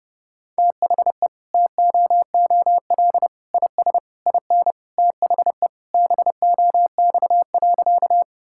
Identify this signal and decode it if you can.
Morse code, heard as a keyed tone, about 30 wpm, 700 Hz. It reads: THE TOOL IS IN THE BOX.
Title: Sentences from Top 1000 Words - Review Part 6 - 30wpm